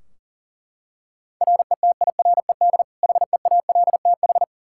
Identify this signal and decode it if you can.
Morse code, heard as a keyed tone, about 40 wpm, 700 Hz. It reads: RETIRED HEALTH
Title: Practice QSO - Exchange 3 - More Info - 40wpm